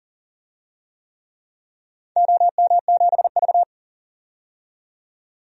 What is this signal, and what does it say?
Morse code, heard as a keyed tone, about 40 wpm, 700 Hz. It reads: OM7V
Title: DXCC-100 Call Signs - 4 Characters - 40wpm